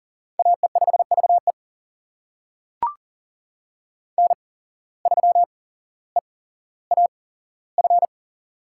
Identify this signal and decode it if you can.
Morse code, heard as a keyed tone, about 40 wpm, spaced as for 17 wpm, 700 Hz. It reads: AE5VE N3EAF
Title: USA Call Signs - 5 Characters - 8x Character Spacing - 40wpm